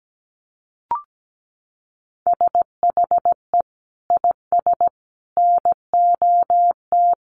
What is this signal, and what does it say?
Morse code, heard as a keyed tone, about 17 wpm, 700 Hz. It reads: SHE IS NOT
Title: Sentences from Top 700 Words 17wpm